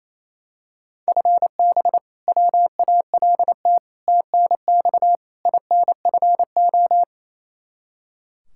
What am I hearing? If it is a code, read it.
Morse code, heard as a keyed tone, about 28 wpm, 700 Hz. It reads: FB WALT TNX INFO
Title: Practice QSO - Exchange 3 - More Info - 28wpm